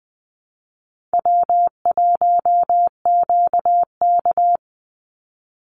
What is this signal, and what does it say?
Morse code, heard as a keyed tone, about 20 wpm, 700 Hz. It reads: W1QK
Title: Easy Call Signs Repeated 3x - 20wpm